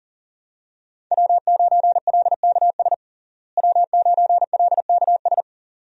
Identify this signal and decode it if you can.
Morse code, heard as a keyed tone, about 37 wpm, 700 Hz. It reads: W9LKS W9LKS
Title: USA Call Signs - 5 Characters - Repeated 2x - 40wpm